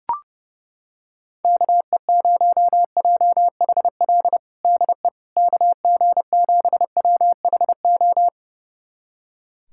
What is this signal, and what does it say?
Morse code, heard as a keyed tone, about 32 wpm, 700 Hz. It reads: KE0JHL DE KG7WHO
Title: USA Callsign DE Callsign 30wpm